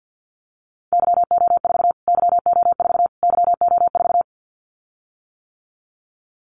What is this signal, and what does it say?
Morse code, heard as a keyed tone, about 49 wpm, 700 Hz. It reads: YO4 YO4 YO4
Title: Callsign Prefixes for Top 25 DXCC - Repeated 3x - 50wpm